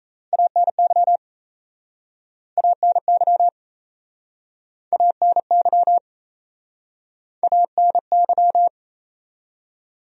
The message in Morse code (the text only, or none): ANY ANY ANY ANY